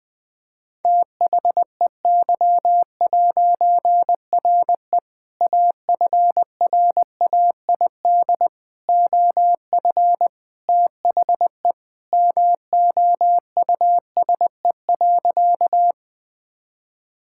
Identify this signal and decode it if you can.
Morse code, heard as a keyed tone, about 20 wpm, 700 Hz. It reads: THEY'RE AFRAID OF THE MOUSE.
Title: All-Age Appropriate Jokes - 20wpm